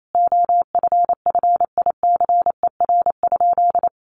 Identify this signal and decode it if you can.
Morse code, heard as a keyed tone, about 28 wpm, 700 Hz. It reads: OFFICER?
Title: Sentences from Top 1200 Words 28wpm